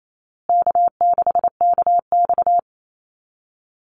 Morse code, heard as a keyed tone, about 28 wpm, 700 Hz. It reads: K6KX
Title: Easy Call Signs Repeated 3x - 28wpm